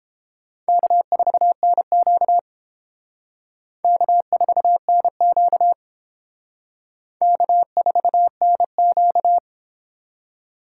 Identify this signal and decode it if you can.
Morse code, heard as a keyed tone, about 29 wpm, 700 Hz. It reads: K4NQ K4NQ K4NQ